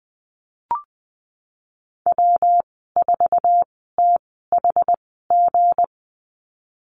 Morse code, heard as a keyed tone, about 20 wpm, spaced as for 15 wpm, 700 Hz. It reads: W4THG
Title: 179 Lesson 29g - Slash - Call Signs 20wpm